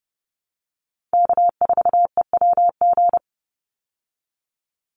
Morse code, heard as a keyed tone, about 30 wpm, 700 Hz. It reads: K4EWG
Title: Call Signs Repeated 2x - 30wpm